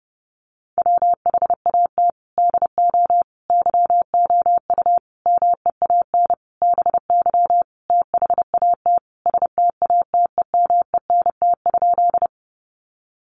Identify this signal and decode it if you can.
Morse code, heard as a keyed tone, about 30 wpm, 700 Hz. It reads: WHAT DO YOU MEAN BY THAT STATEMENT?
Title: Sentences from Top 1200 Words 30wpm